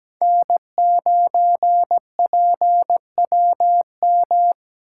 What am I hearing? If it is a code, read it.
Morse code, heard as a keyed tone, about 17 wpm, 700 Hz. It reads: N9PWM
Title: USA Call Signs - 2x Character Spacing - 17wpm